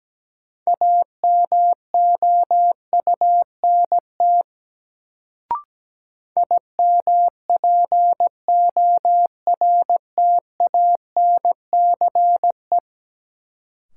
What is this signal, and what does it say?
Morse code, heard as a keyed tone, about 17 wpm, 700 Hz. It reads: AMOUNT IMPORTANCE
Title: Sets of 3 Words - New Word Drill - 17wpm